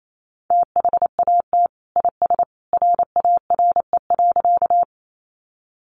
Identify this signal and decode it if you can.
Morse code, heard as a keyed tone, about 28 wpm, 700 Hz. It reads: THAT IS RARE.
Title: Sentences from Top 1100 Words 28wpm